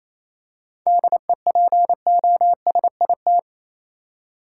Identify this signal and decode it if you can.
Morse code, heard as a keyed tone, about 28 wpm, 700 Hz. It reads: DEPOSIT
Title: Seven Letter Words - 28wpm